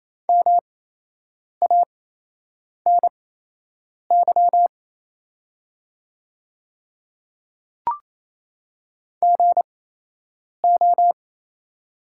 Morse code, heard as a keyed tone, about 28 wpm, spaced as for 8 wpm, 700 Hz. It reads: MANY GO
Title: 8x Character Spacing - Top 100 Words - 28wpm